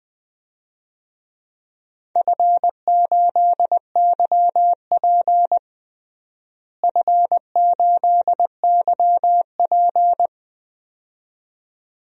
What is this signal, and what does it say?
Morse code, heard as a keyed tone, about 20 wpm, 700 Hz. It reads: F8YP F8YP
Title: DXCC-100 Call Signs - 4 Characters - Repeated 2x - 20wpm